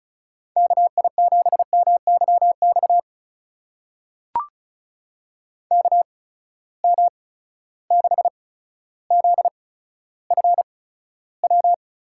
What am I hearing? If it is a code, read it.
Morse code, heard as a keyed tone, about 36 wpm, spaced as for 15 wpm, 700 Hz. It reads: KI7MYX KM6ZFW
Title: USA Call Signs - 6 Characters - 8x Character Spacing - 35wpm